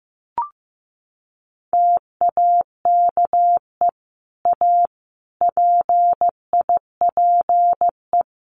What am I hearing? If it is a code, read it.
Morse code, heard as a keyed tone, about 14 wpm, 700 Hz. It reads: TAKE A PIPE
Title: Sentences from Top 1500 Words 15wpm